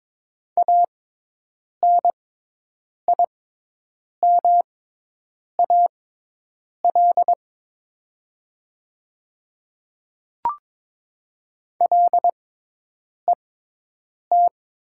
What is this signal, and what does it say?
Morse code, heard as a keyed tone, about 22 wpm, spaced as for 8 wpm, 700 Hz. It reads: ANIMAL LET
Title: 6x Character Spacing - Top 300 Words - 22wpm